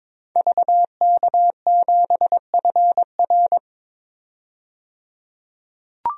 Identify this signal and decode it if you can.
Morse code, heard as a keyed tone, about 22 wpm, 700 Hz. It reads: VK7FR
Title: DXCC-100 Call Signs - 5 Characters - 22wpm